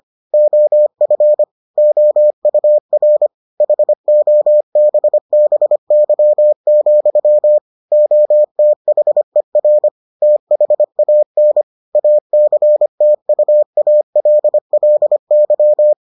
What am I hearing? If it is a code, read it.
Morse code, heard as a keyed tone, about 25 wpm, 600 Hz. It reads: OF OUR HOBBY, OTHER THAN ACTUALLY